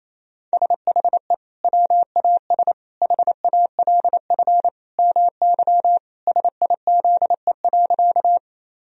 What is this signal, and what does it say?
Morse code, heard as a keyed tone, about 28 wpm, 700 Hz. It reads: SHE WAS HALF MY SIZE.
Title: Sentences from Top 1000 Words - Review Part 2 - 28wpm